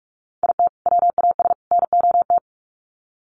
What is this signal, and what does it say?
Morse code, heard as a keyed tone, about 44 wpm, 700 Hz. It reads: IT WAS NOT
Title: Sets of 3 Words 45wpm